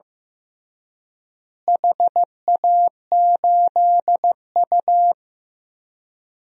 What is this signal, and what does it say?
Morse code, heard as a keyed tone, about 15 wpm, 700 Hz. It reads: HA8U
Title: DXCC-100 Call Signs - 4 Characters - Repeated 3x - 15wpm